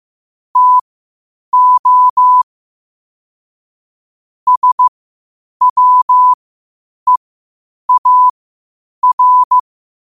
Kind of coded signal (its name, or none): Morse code